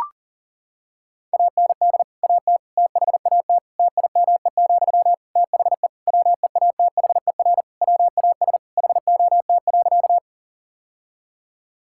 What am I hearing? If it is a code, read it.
Morse code, heard as a keyed tone, about 37 wpm, 700 Hz. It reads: AND AT THAT TIME, THE WEATHER WAS HOT.
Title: Sentences from Top 1000 Words - Review Part 5 - 40wpm